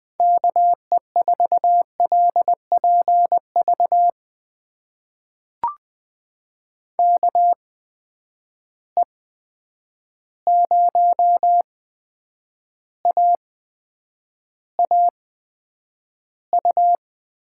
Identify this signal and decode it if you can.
Morse code, heard as a keyed tone, about 20 wpm, spaced as for 9 wpm, 700 Hz. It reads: KE4LPV KE0AAU
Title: USA Call Signs - 6 Characters - 8x Character Spacing - 20wpm